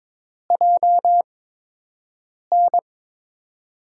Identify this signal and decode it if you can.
Morse code, heard as a keyed tone, about 22 wpm, spaced as for 6 wpm, 700 Hz. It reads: JN
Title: USA Call Signs - 8x Character Spacing - 22wpm